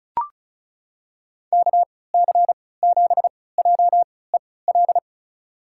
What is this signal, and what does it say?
Morse code, heard as a keyed tone, about 36 wpm, spaced as for 20 wpm, 700 Hz. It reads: KC7JEL
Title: USA Call Signs - 3x Character Spacing - 35wpm